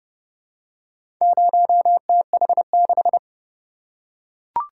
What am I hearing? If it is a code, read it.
Morse code, heard as a keyed tone, about 30 wpm, 700 Hz. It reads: 0TH6